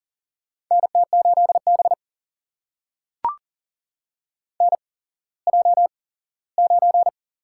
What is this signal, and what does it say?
Morse code, heard as a keyed tone, about 40 wpm, spaced as for 19 wpm, 700 Hz. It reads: NT8B NJ9
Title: USA Call Signs - 4 Characters - 8x Character Spacing - 40wpm